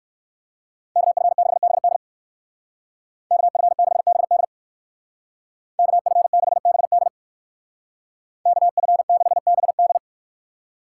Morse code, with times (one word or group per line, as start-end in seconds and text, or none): 1.0-2.0 KF6BD
3.3-4.4 KF6BD
5.8-7.1 KF6BD
8.5-10.0 KF6BD